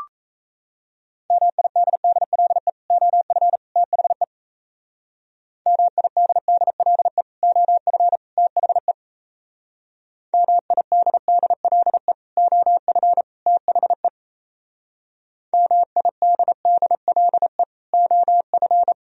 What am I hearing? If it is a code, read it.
Morse code, heard as a keyed tone, about 35 wpm, 700 Hz. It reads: MIDDLE OF THE MIDDLE OF THE MIDDLE OF THE MIDDLE OF
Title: Speed-Racing - Sets of 3 Words 28wpm